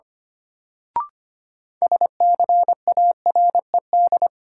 Morse code, heard as a keyed tone, about 25 wpm, 700 Hz. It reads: SCARED
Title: Sentences from Top 700 Words - New Word Drill - 25wpm